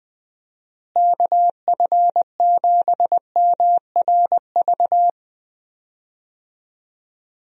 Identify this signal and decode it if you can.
Morse code, heard as a keyed tone, about 21 wpm, 700 Hz. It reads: KF7MRV
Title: USA Call Signs - 6 Characters - Repeated 2x - 20wpm